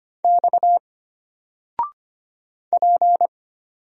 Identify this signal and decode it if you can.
Morse code, heard as a keyed tone, about 25 wpm, 700 Hz. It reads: X P
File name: Single-Letter-Number-25wpm